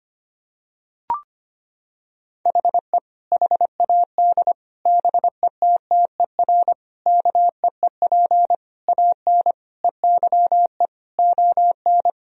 Morse code, heard as a keyed tone, about 25 wpm, 700 Hz. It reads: HE HAD BETTER KEEP AN EYE ON
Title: Sentences from Top 1000 Words - Review Part 1 - 25wpm